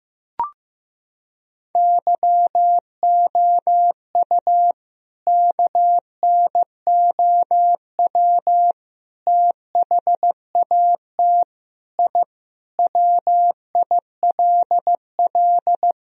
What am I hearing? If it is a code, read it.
Morse code, heard as a keyed tone, about 15 wpm, 700 Hz. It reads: YOU KNOW THAT I WILL
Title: Sentences from Top 1000 Words - Review Part 1 - 15wpm